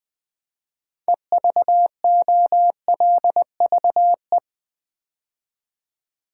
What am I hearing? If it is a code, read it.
Morse code, heard as a keyed tone, about 20 wpm, 700 Hz. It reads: EVOLVE
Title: 6 Letter Words - First 3 Letters Given - 20wpm